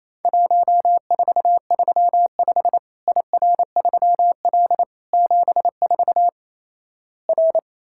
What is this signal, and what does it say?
Morse code, heard as a keyed tone, about 28 wpm, 700 Hz. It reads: 1435 IR3L 74 R
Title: Worked All Europe DX Contest - QTC Exchange - 28wpm